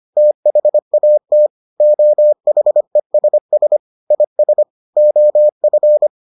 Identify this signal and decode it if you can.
Morse code, heard as a keyed tone, about 25 wpm, 600 Hz. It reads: THAT OHESS IS OF